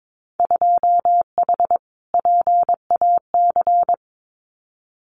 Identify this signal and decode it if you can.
Morse code, heard as a keyed tone, about 22 wpm, 700 Hz. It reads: 2H PAC
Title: Winter Field Day Exchanges Encore 22wpm